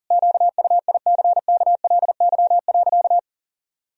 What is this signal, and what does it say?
Morse code, heard as a keyed tone, about 40 wpm, 700 Hz. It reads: QUICKLY.
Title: Sentences from Top 1000 Words - Review Part 1 - 40wpm